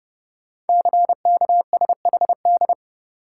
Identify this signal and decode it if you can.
Morse code, heard as a keyed tone, about 30 wpm, 700 Hz. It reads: CKSHD
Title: Code Groups - 5 Letters - 30wpm